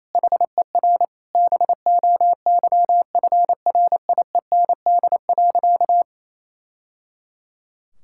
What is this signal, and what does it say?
Morse code, heard as a keyed tone, about 27 wpm, 700 Hz. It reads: HER BOYFRIEND.
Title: Sentences from Top 1000 Words - Review Part 5 - 28wpm